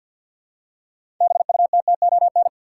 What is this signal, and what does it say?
Morse code, heard as a keyed tone, about 52 wpm, 700 Hz. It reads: BUTTON
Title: Six Letter Words - 50wpm